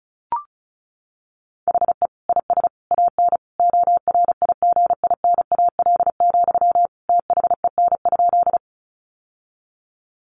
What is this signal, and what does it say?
Morse code, heard as a keyed tone, about 34 wpm, 700 Hz. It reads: HE IS AN ORIGINAL, THEN?
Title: Sentences from Top 900 Words 35wpm